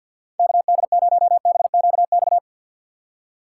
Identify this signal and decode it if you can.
Morse code, heard as a keyed tone, about 50 wpm, 700 Hz. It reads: KD0BQX
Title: USA Call Signs - 6 Characters - 3x Character Spacing - 50wpm